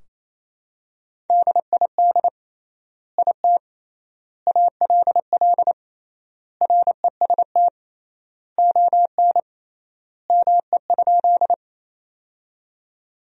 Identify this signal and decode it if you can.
Morse code, heard as a keyed tone, about 28 wpm, 700 Hz. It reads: DID IT ALL REST ON ME?
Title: 3x Word-Spacing - Sentences from Top 500 Words - 28wpm